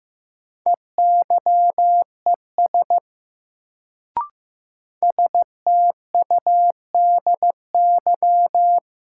Morse code, heard as a keyed tone, about 15 wpm, 700 Hz. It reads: EYES STUDY
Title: Sets of 5 Words - New Word Drill - 15wpm